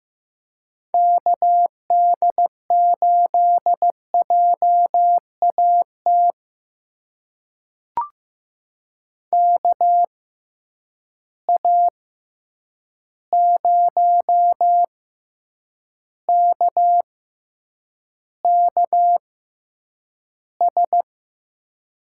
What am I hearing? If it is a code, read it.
Morse code, heard as a keyed tone, about 15 wpm, spaced as for 8 wpm, 700 Hz. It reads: KD8JAT KA0KKS
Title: USA Call Signs - 6 Characters - 6x Character Spacing - 15wpm